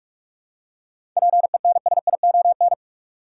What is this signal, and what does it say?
Morse code, heard as a keyed tone, about 46 wpm, 700 Hz. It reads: PENSION
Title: Seven Letter Words - 45wpm